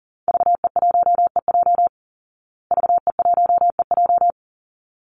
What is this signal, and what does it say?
Morse code, heard as a keyed tone, about 40 wpm, 700 Hz. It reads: VE1EJ VE1EJ
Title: Call Signs Repeated 3x - 40wpm